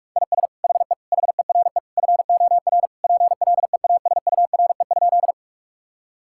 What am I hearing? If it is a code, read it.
Morse code, heard as a keyed tone, about 46 wpm, 700 Hz. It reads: IS HE HERE FOR PLEASURE?
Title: Sentences from Top 1600 Words 45wpm